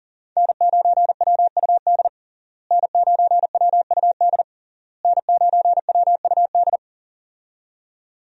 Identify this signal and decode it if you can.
Morse code, heard as a keyed tone, about 40 wpm, 700 Hz. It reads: N9WUD N9WUD N9WUD
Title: USA Call Signs - 5 Characters - Repeated 3x - 40wpm